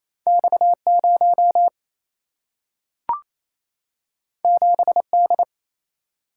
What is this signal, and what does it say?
Morse code, heard as a keyed tone, about 28 wpm, 700 Hz. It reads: X0 7D
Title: Single-Letter Single-Number 28wpm